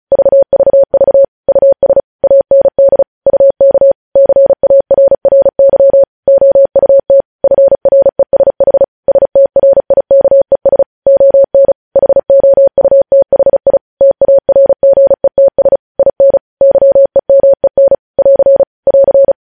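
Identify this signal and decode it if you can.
Morse code, heard as a keyed tone, about 36 wpm, 550 Hz. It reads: VVV US AND UK CARRY OUT FRESH STRIKES ON HOUTHI TARGETS IN YEMEN <AR> <AR>
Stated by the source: Morse code transmission of news headlines, preceded by vvv.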